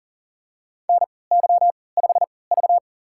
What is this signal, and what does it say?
Morse code, heard as a keyed tone, about 40 wpm, spaced as for 23 wpm, 700 Hz. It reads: NY5V
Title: USA Call Signs - 4 Characters - 3x Character Spacing - 40wpm